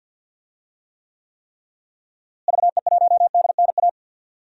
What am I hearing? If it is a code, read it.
Morse code, heard as a keyed tone, about 51 wpm, 700 Hz. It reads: VE1DNA
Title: DXCC-100 Call Signs - 6 Characters - Repeated 3x - 50wpm